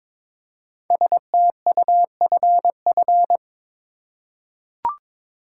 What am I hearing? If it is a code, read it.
Morse code, heard as a keyed tone, about 22 wpm, 700 Hz. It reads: STUFF
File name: 5 Letter Words - First 3 Letters Given - 22wpm